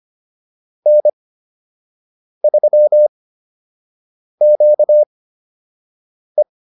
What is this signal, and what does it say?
Morse code, heard as a keyed tone, about 25 wpm, spaced as for 6 wpm, 600 Hz. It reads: N3QE